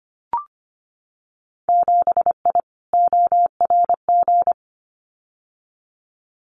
Morse code, heard as a keyed tone, about 25 wpm, 700 Hz. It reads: 7I ORG
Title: Winter Field Day Exchanges Encore 25wpm